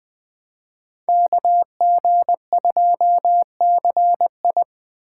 Morse code, heard as a keyed tone, about 20 wpm, 700 Hz. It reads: KG2CI
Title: Speed Racing - USA Call Signs 17wpm